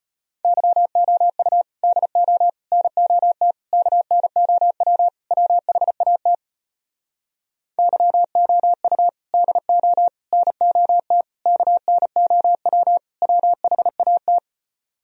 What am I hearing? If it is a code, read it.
Morse code, heard as a keyed tone, about 33 wpm, 700 Hz. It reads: YOU DO NOT KNOW WHAT YOU DO NOT KNOW WHAT